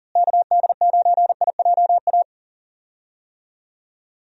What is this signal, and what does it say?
Morse code, heard as a keyed tone, about 40 wpm, 700 Hz. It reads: KD9IJA
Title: USA Call Signs - 6 Characters - Repeated 3x - 40wpm